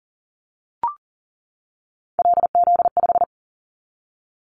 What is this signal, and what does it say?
Morse code, heard as a keyed tone, about 37 wpm, 700 Hz. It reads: LZ5
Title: Callsign Prefixes for Top 25 DXCC - 40wpm